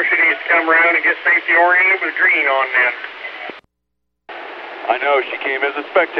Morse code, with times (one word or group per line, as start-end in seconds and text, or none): none